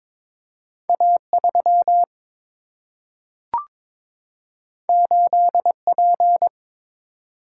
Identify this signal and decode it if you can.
Morse code, heard as a keyed tone, about 22 wpm, 700 Hz. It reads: A3 8P
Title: Single-Letter Single-Number 22wpm